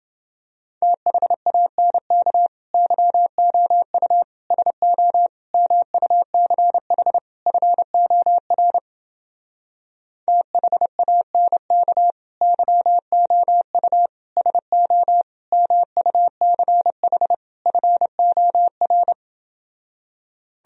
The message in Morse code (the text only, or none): THANK YOU SO MUCH FOR THANK YOU SO MUCH FOR